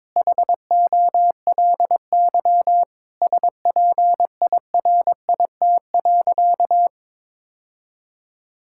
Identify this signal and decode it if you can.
Morse code, heard as a keyed tone, about 22 wpm, 700 Hz. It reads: HOLY SPIRIT.
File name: Sentences with 9-Word Phrases - 22wpm